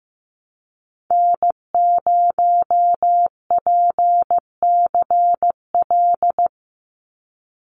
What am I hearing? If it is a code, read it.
Morse code, heard as a keyed tone, about 15 wpm, 700 Hz. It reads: N0PCL
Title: NA SOTA Activator Callsigns 15wpm